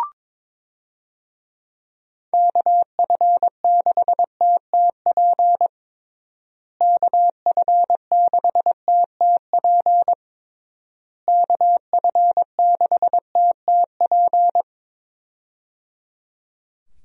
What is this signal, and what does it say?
Morse code, heard as a keyed tone, about 22 wpm, 700 Hz. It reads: KF6TTP KF6TTP KF6TTP
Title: USA Call Signs - 6 Characters - Repeated 3x - 22wpm